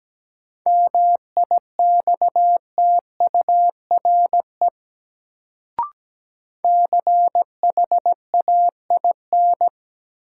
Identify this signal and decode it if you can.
Morse code, heard as a keyed tone, about 17 wpm, 700 Hz. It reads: MIXTURE CHAIN
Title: Sentences from Top 1500 Words - New Word Drill Alt - 17wpm